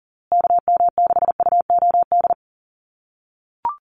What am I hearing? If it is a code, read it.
Morse code, heard as a keyed tone, about 40 wpm, 700 Hz. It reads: KM6UOD
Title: USA Call Signs Repeated 2x 40wpm